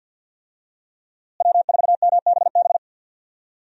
Morse code, heard as a keyed tone, about 46 wpm, 700 Hz. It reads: W4MBB
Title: USA Call Signs - 5 Characters - Repeated 2x - 50wpm